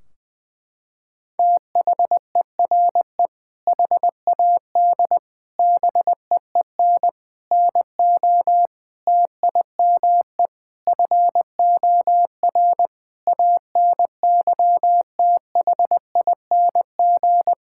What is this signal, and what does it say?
Morse code, heard as a keyed tone, about 21 wpm, 700 Hz. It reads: THERE HAD BEEN NO TIME FOR ANYTHING
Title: Sentences from Top 1600 Words 20wpm